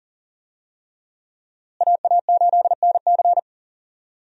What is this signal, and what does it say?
Morse code, heard as a keyed tone, about 40 wpm, 700 Hz. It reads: AA8NC
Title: USA Call Signs - 5 Characters - Repeated 3x - 40wpm